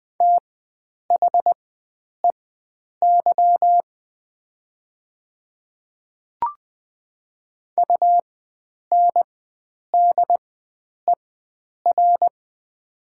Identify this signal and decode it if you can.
Morse code, heard as a keyed tone, about 20 wpm, spaced as for 9 wpm, 700 Hz. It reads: THEY UNDER
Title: 4x Character Spacing - Top 300 Words - 20wpm